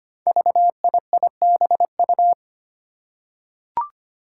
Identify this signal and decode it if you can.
Morse code, heard as a keyed tone, about 25 wpm, 700 Hz. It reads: VIIBU
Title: Five Challenging Letter-Number Combinations 25wpm